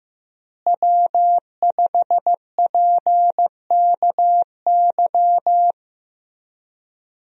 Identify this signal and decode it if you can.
Morse code, heard as a keyed tone, about 15 wpm, 700 Hz. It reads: W5PKY
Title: USA Call Signs - 5 Characters - Repeated 3x - 15wpm